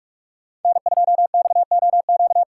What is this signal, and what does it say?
Morse code, heard as a keyed tone, about 57 wpm, 700 Hz. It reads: N2XOQ